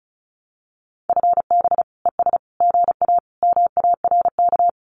Morse code, heard as a keyed tone, about 34 wpm, 700 Hz. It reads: FB ES GA MARK
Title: Common QSO Phrases Encore 35wpm